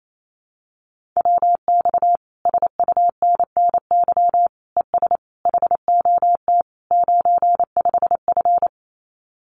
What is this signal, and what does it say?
Morse code, heard as a keyed tone, about 28 wpm, 700 Hz. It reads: WX SUNNY ES HOT 95F
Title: Common QSO Phrases Encore 28wpm